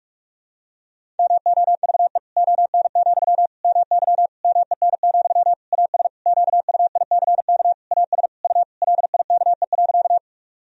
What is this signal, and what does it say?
Morse code, heard as a keyed tone, about 41 wpm, 700 Hz. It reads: MOVE ON, MY MEN, AS QUICK AS U LIKE.
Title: Sentences from Top 500 Words and Abbreviations 45wpm